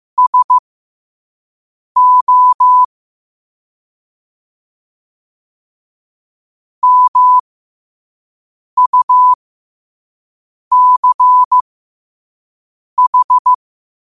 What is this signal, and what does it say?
Morse code, heard as a keyed tone, about 15 wpm, spaced as for 5 wpm, 1000 Hz. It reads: SO MUCH
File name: qotdfiles / 5wpm